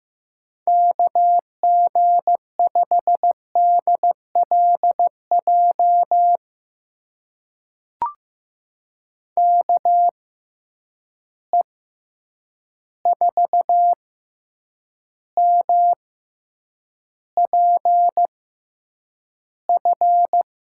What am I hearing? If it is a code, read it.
Morse code, heard as a keyed tone, about 15 wpm, spaced as for 8 wpm, 700 Hz. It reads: KG5DLJ KE4MPF
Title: USA Call Signs - 6x Character Spacing - 15wpm